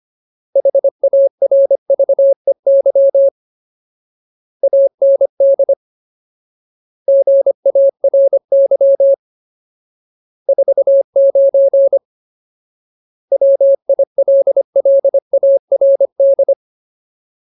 Morse code, heard as a keyed tone, about 25 wpm, 550 Hz. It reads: HARVEY AND GARY 49 WILLARD
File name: Ultimate Word List / 25WPM Ultimate